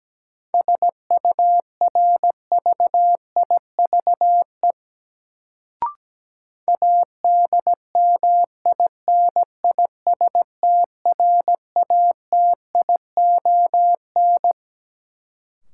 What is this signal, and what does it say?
Morse code, heard as a keyed tone, about 17 wpm, 700 Hz. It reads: SURVIVE ADMINISTRATION
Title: Sentences from Top 1300 Words - New Word Drill Alt - 17wpm